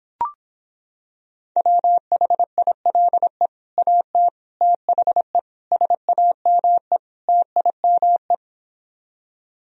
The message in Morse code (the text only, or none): WHILE AT THE SAME TIME